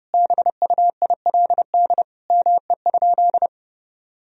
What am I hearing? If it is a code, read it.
Morse code, heard as a keyed tone, about 32 wpm, 700 Hz. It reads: BUILD ME?
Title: Sentences from Top 1000 Words - Review Part 1 - 30wpm